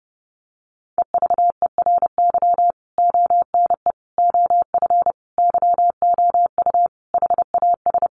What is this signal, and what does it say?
Morse code, heard as a keyed tone, about 30 wpm, 700 Hz. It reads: EVERY ONE OF YOU HAS
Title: Sentences from Top 1500 Words 30wpm